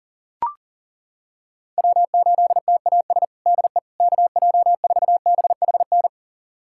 Morse code, heard as a keyed tone, about 40 wpm, 700 Hz. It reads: W8TAS DE KJ4BHN
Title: USA Callsign DE Callsign 40wpm